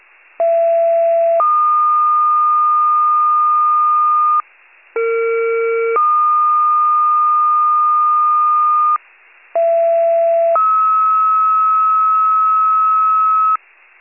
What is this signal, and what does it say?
Two-tone page, heard as a one-second tone, then a three-second tone, about 670 then 1220 Hz, about 460 then 1220 Hz, about 670 then 1290 Hz. It is heard from 0.4 to 13.6 s.